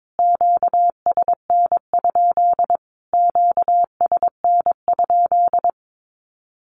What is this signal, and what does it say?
Morse code, heard as a keyed tone, about 22 wpm, 700 Hz. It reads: QSN? QSN?
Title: Q-Codes Repeated 3x 22wpm